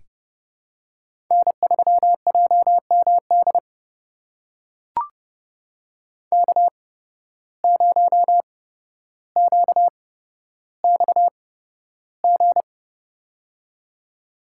Morse code, heard as a keyed tone, about 31 wpm, spaced as for 13 wpm, 700 Hz. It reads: N3JMD K0QXG
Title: USA Call Signs - 5 Characters - 8x Character Spacing - 30wpm